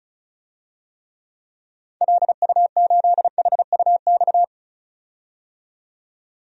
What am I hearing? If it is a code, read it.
Morse code, heard as a keyed tone, about 34 wpm, 700 Hz. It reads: LU8HUX
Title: DXCC-100 Call Signs - 6 Characters - 35wpm